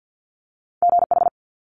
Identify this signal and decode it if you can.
Morse code, heard as a keyed tone, about 50 wpm, 700 Hz. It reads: GH
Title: Multi-letter Phonemes Quick Fire 50wpm